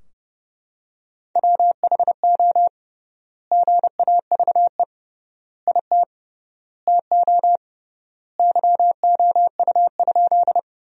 Morse code, heard as a keyed tone, about 31 wpm, 700 Hz. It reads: WHO GAVE IT TO YOU?
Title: 3x Word-Spacing - Sentences from Top 500 Words - 30wpm